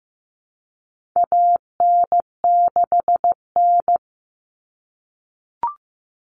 Morse code, heard as a keyed tone, about 15 wpm, 700 Hz. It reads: AN6N
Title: Common Cut Numbers - 4-Digit 15wpm